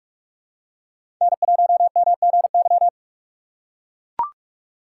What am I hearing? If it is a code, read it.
Morse code, heard as a keyed tone, about 42 wpm, 700 Hz. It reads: N1MGY
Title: USA Call Signs - 3x Character Spacing - 45wpm